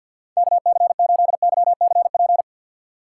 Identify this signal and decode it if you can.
Morse code, heard as a keyed tone, about 52 wpm, 700 Hz. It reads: KC8YCP
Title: USA Call Signs - 2x Character Spacing - 50wpm